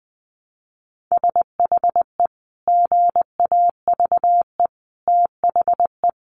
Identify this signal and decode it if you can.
Morse code, heard as a keyed tone, about 20 wpm, 700 Hz. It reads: SHE GAVE THE
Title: Sentences from Top 900 Words 20wpm